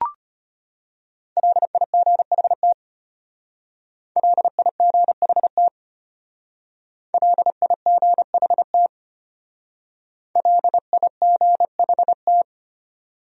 Morse code, heard as a keyed tone, about 32 wpm, 700 Hz. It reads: LIGHT LIGHT LIGHT LIGHT